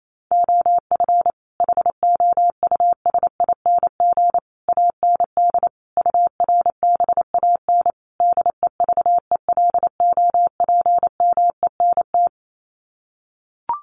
Morse code, heard as a keyed tone, about 28 wpm, 700 Hz. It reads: OF HOUSING AND URBAN DEVELOPMENT
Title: Sets of 5 Words Encore2 28wpm